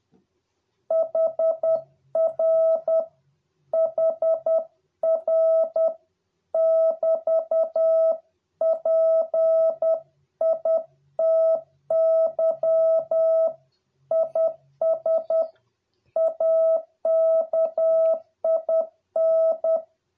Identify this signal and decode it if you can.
Morse code, heard as a keyed tone, about 10 wpm, 650 Hz. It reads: HR HR <BT> PITY IS AKIN